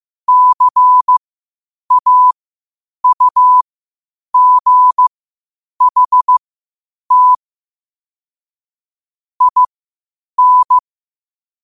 Morse code, heard as a keyed tone, about 15 wpm, spaced as for 8 wpm, 1000 Hz. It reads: CAUGHT IN
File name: qotdfiles / 8wpm